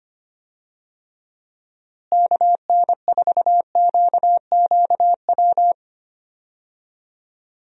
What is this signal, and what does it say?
Morse code, heard as a keyed tone, about 25 wpm, 700 Hz. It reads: KN4QQW
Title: USA Call Signs - 6 Characters - Repeated 2x - 25wpm